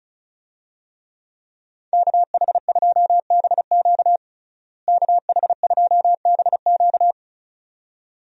Morse code, heard as a keyed tone, about 36 wpm, 700 Hz. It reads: KH2BQ KH2BQ
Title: USA Call Signs - 5 Characters - Repeated 2x - 35wpm